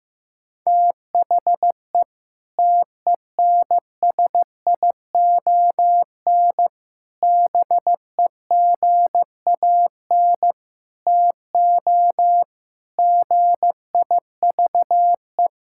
Morse code, heard as a keyed tone, about 16 wpm, 700 Hz. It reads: THE TENSION BEGAN TO GIVE
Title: Sentences from Top 1600 Words 15wpm